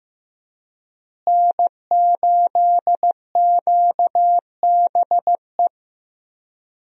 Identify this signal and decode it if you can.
Morse code, heard as a keyed tone, about 15 wpm, 700 Hz. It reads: N8QBE
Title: USA Call Signs - 5 Characters - Repeated 2x - 15wpm